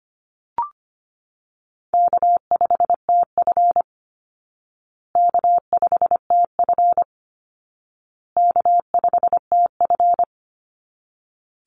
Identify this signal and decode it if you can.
Morse code, heard as a keyed tone, about 25 wpm, 700 Hz. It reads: K5TF K5TF K5TF
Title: Call Signs Repeated 2x - 25wpm